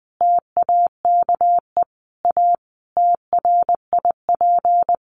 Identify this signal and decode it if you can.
Morse code, heard as a keyed tone, about 20 wpm, 700 Hz. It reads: TAKE A TRIP
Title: Sentences from Top 1300 Words 20wpm